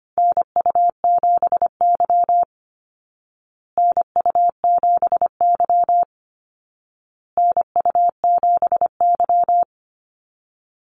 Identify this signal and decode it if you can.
Morse code, heard as a keyed tone, about 25 wpm, 700 Hz. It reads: NU7Y NU7Y NU7Y
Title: Call Signs Repeated 2x - 25wpm